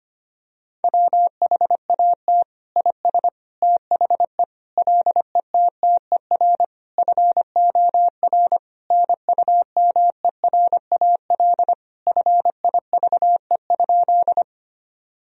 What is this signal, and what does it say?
Morse code, heard as a keyed tone, about 25 wpm, 700 Hz. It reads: WHAT IS THE LETTER FOR NUMERAL FIVE?
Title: Sentences from Top 1000 Words - Review Part 2 - 25wpm